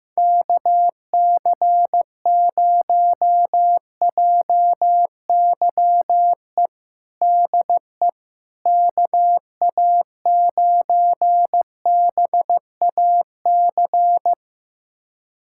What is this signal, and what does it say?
Morse code, heard as a keyed tone, about 15 wpm, 700 Hz. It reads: KC0JYE DE KA9BAC
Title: USA Callsign DE Callsign - 6 Characters - 15wpm